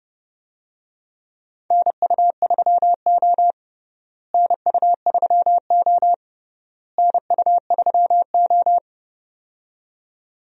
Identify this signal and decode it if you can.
Morse code, heard as a keyed tone, about 32 wpm, 700 Hz. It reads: NU3O NU3O NU3O
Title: USA Call Signs - 4 Characters - Repeated 3x - 30wpm